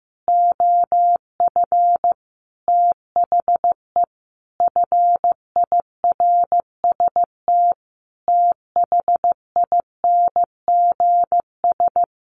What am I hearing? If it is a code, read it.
Morse code, heard as a keyed tone, about 15 wpm, 700 Hz. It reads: OF THE FIRST THINGS
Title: Sets of 5 Words Encore 15wpm